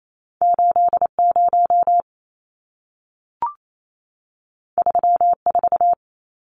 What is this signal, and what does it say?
Morse code, heard as a keyed tone, about 28 wpm, 700 Hz. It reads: 80 34
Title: Numbers - 2-Digit 28wpm